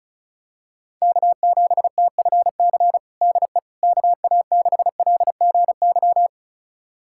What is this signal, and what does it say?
Morse code, heard as a keyed tone, about 34 wpm, 700 Hz. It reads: K7TFC DE KA6LGY
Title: USA Callsign DE Callsign 35wpm